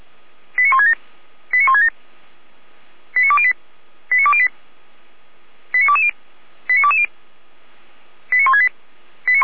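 Five-tone selective call (ZVEI), heard as quick runs of five tones, tones of about 70 ms. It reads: 89167 89167 89208 89208 89220 89220 89168
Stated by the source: Received on an SDR; ZVEI five-tone selective call 89099, fire brigade Orlamünde (feuerwehr-orlamuende).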